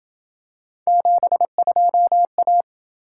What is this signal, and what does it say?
Morse code, heard as a keyed tone, about 26 wpm, 700 Hz. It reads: 72A